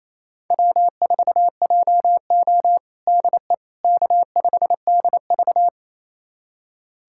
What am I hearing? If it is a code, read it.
Morse code, heard as a keyed tone, about 28 wpm, 700 Hz. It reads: W4JO DE K5DV
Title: USA Callsign DE Callsign - 4 Characters - 28wpm